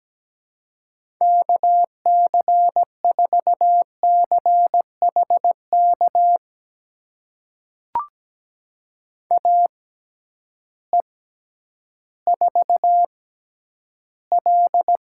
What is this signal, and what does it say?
Morse code, heard as a keyed tone, about 17 wpm, spaced as for 10 wpm, 700 Hz. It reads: KC4CHK AE4L
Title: USA Call Signs - 6x Character Spacing - 17wpm